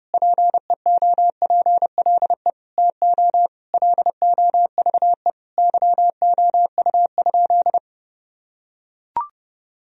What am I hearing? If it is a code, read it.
Morse code, heard as a keyed tone, about 32 wpm, 700 Hz. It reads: PEOPLE TO LOVE YOU?
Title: Sentences from Top 1000 Words - Review Part 3 - 30wpm